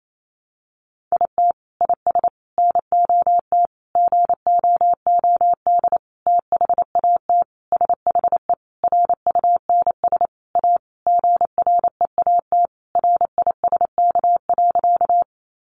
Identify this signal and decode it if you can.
Morse code, heard as a keyed tone, about 28 wpm, 700 Hz. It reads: IT IS NOT GOOD THAT SHE RUNS A GREAT RISK.
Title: Sentences from Top 600 Words 28wpm